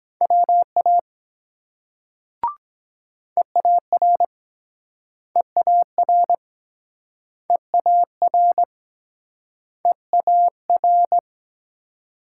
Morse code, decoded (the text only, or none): WA EAR EAR EAR EAR